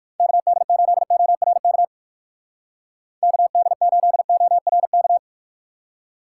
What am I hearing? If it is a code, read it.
Morse code, heard as a keyed tone, about 56 wpm, 700 Hz. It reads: KD8ORK KD8ORK
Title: Speed Racing - USA Call Signs 45wpm